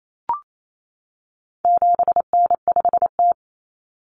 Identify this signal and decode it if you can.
Morse code, heard as a keyed tone, about 28 wpm, 700 Hz. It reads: 7N5T
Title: Common Cut Numbers - 4-Digit 28wpm